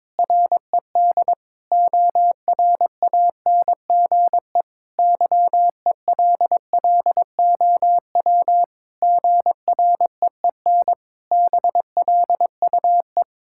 Morse code, heard as a keyed tone, about 22 wpm, 700 Hz. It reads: RED ORANGE YELLOW GREEN BLUE
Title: Pangram Challenge v2 - 22wpm